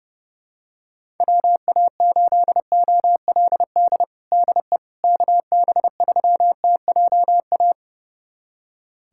Morse code, heard as a keyed tone, about 30 wpm, 700 Hz. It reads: WA8OLD DE KB3TJA
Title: USA Callsign DE Callsign - 6 Characters - 30wpm